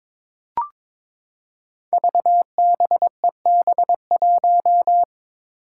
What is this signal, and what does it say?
Morse code, heard as a keyed tone, about 22 wpm, 700 Hz. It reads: VBEB1
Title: Five Challenging Letter-Number Combinations 22wpm